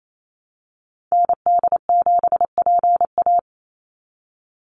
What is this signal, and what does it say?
Morse code, heard as a keyed tone, about 28 wpm, 700 Hz. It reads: ND7PA
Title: NA SOTA Activator Callsigns 28wpm